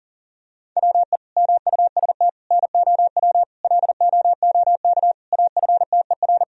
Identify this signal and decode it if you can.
Morse code, heard as a keyed tone, about 37 wpm, 700 Hz. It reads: WE MUST NOW LOOK AFTER
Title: Sentences from Top 1000 Words - Review Part 4 - 40wpm